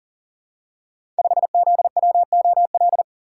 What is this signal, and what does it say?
Morse code, heard as a keyed tone, about 39 wpm, 690 Hz. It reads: 5ZWOL
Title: Code Groups - 5 Letter:Numbers - 40wpm